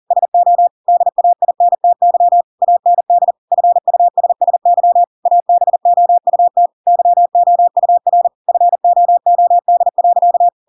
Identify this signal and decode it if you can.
Morse code, heard as a keyed tone, about 40 wpm, 700 Hz. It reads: SO DAINTY AND FUSSY ABOUT YOUR FOOD.
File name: Pinocchio / Pinocchio ch7 40 40 ik1hge